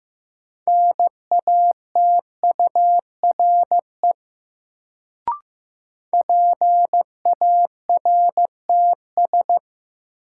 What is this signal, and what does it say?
Morse code, heard as a keyed tone, about 15 wpm, 700 Hz. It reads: NATURE PARTS
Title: Sets of 3 Words - New Word Drill - 15wpm